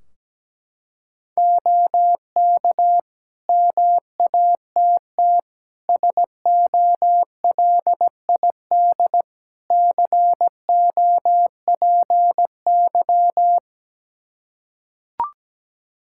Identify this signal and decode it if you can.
Morse code, heard as a keyed tone, about 17 wpm, 700 Hz. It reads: OK MATT SOLID COPY
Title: Practice QSO - Exchange 3 - More Info - 17wpm